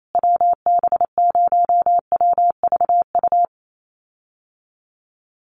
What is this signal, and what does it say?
Morse code, heard as a keyed tone, about 28 wpm, 700 Hz. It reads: WB0WVU
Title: USA Call Signs Repeated 2x 28wpm